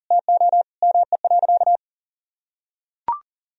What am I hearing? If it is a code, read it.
Morse code, heard as a keyed tone, about 37 wpm, 700 Hz. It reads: TO ME.
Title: Sentences from Top 200 Words and Abbreviations 40wpm